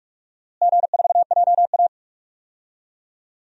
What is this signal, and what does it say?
Morse code, heard as a keyed tone, about 45 wpm, 700 Hz. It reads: G4JA
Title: DXCC-100 Call Signs - 4 Characters - Repeated 3x - 45wpm